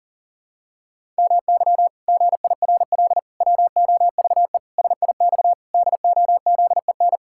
Morse code, heard as a keyed tone, about 40 wpm, 700 Hz. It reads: MY GIRL WOVE SIX DOZEN
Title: Pangram Challenge v2 - 40wpm